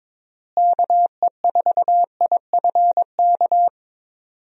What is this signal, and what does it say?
Morse code, heard as a keyed tone, about 22 wpm, 700 Hz. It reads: KE4IFK
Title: USA Call Signs - 6 Characters - Repeated 3x - 22wpm